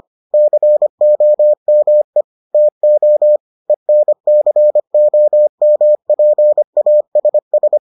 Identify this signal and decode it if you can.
Morse code, heard as a keyed tone, about 25 wpm, 600 Hz. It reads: COME TO ENCOMPASS